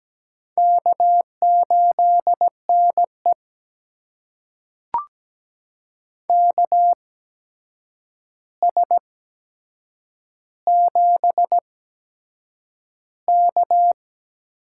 Morse code, heard as a keyed tone, about 17 wpm, spaced as for 7 wpm, 700 Hz. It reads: K8NE KS7K
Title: USA Call Signs - 4 Characters - 8x Character Spacing - 17wpm